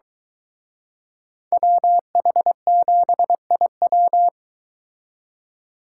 Morse code, heard as a keyed tone, about 23 wpm, 700 Hz. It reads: WH7IW